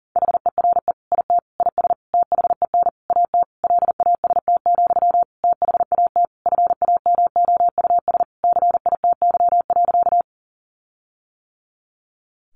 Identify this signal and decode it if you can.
Morse code, heard as a keyed tone, about 40 wpm, 700 Hz. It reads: HERE IT IS THEN AT LAST, THAT FAMOUS CITY.
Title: Sentences from Top 1500 Words 40wpm